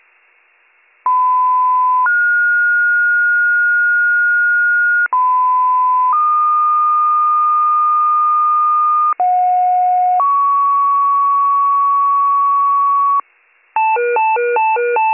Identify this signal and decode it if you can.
Two-tone page, heard as a one-second tone, then a three-second tone, about 1010 then 1430 Hz, about 1010 then 1220 Hz, about 730 then 1120 Hz. It is heard from 1.0 to 13.2 s.